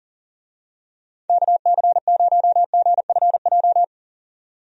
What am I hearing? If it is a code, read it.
Morse code, heard as a keyed tone, about 40 wpm, 700 Hz. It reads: KC0GFJ
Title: USA Call Signs - 6 Characters - 40wpm